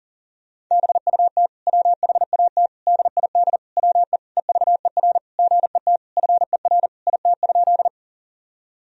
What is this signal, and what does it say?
Morse code, heard as a keyed tone, about 37 wpm, 700 Hz. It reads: BUT WHAT DID WE EVER GET FER IT?
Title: Sentences from Top 400 Words and Abbreviations 40wpm